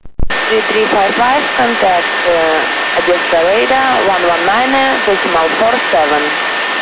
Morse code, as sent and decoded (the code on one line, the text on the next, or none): none